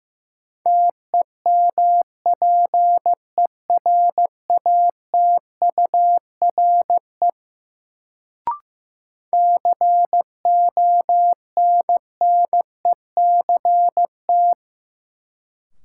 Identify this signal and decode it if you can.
Morse code, heard as a keyed tone, about 16 wpm, 700 Hz. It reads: TEMPERATURE CONNECT
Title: Sentences from Top 1500 Words - New Word Drill Alt - 15wpm